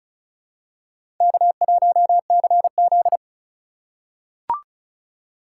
Morse code, heard as a keyed tone, about 36 wpm, 700 Hz. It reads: K1CZ
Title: USA Call Signs - 8x Character Spacing - 35wpm